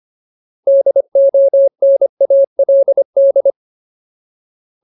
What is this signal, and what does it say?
Morse code, heard as a keyed tone, about 25 wpm, 550 Hz. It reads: DONALD
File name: Ultimate Word List / 25WPM Ultimate